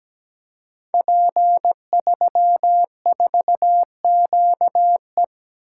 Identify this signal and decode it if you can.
Morse code, heard as a keyed tone, about 17 wpm, 700 Hz. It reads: P34QE
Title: Code Groups - 5 Letter:Numbers - 17wpm